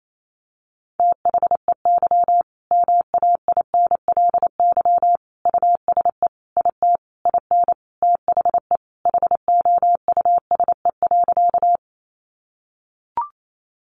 Morse code, heard as a keyed tone, about 28 wpm, 700 Hz. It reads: THEY MAINLY USE IT IN THE HOUSE.
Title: Sentences from Top 1400 Words 28wpm